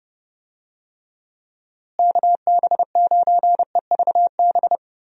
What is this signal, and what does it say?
Morse code, heard as a keyed tone, about 32 wpm, 700 Hz. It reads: KB9EVB
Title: USA Call Signs - 6 Characters - Repeated 3x - 30wpm